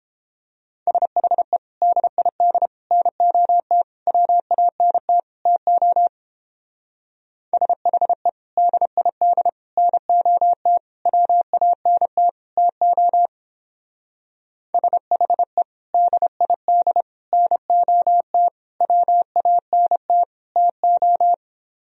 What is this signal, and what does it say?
Morse code, heard as a keyed tone, about 29 wpm, 700 Hz. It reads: SHE DID NOT WANT TO SHE DID NOT WANT TO SHE DID NOT WANT TO